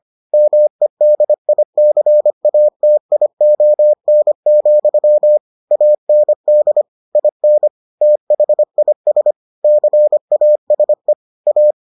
Morse code, heard as a keyed tone, about 25 wpm, 600 Hz. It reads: MEDICATION, AND IN THIS CASE A